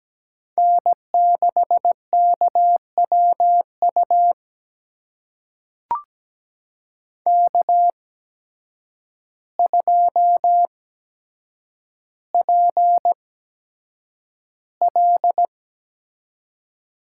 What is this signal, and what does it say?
Morse code, heard as a keyed tone, about 17 wpm, spaced as for 8 wpm, 700 Hz. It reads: N6KWU K2PL
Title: USA Call Signs - 5 Characters - 8x Character Spacing - 17wpm